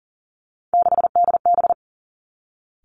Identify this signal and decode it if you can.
Morse code, heard as a keyed tone, about 40 wpm, 700 Hz. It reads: 6DB
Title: Three Challenging Letter-Number Combinations 40wpm